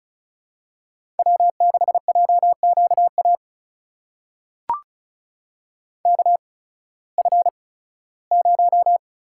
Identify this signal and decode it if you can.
Morse code, heard as a keyed tone, about 36 wpm, spaced as for 19 wpm, 700 Hz. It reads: W6JQA KF0
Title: USA Call Signs - 8x Character Spacing - 35wpm